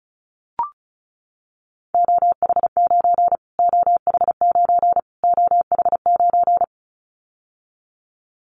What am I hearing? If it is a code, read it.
Morse code, heard as a keyed tone, about 34 wpm, 700 Hz. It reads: OH9 OH9 OH9
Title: Callsign Prefixes for Top 25 DXCC - Repeated 3x - 35wpm